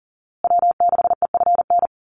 Morse code, heard as a keyed tone, about 40 wpm, 700 Hz. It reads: W6EFN
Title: USA Call Signs Repeated 3x 40wpm